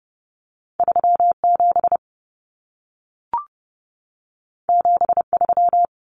Morse code, heard as a keyed tone, about 30 wpm, 700 Hz. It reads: 37 73
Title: Common Cut Numbers - 2-Digit 30wpm